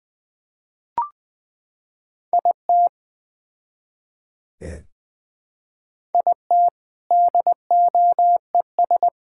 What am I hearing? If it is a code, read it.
Morse code, heard as a keyed tone, about 20 wpm, 700 Hz. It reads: IT IT DOES
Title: Build A Phrase - 3-Word Phrases - 20wpm